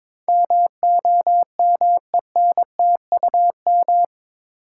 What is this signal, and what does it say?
Morse code, heard as a keyed tone, about 22 wpm, 700 Hz. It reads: MOMENTUM
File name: practice / 22wpm / Eight Letter Words - 22wpm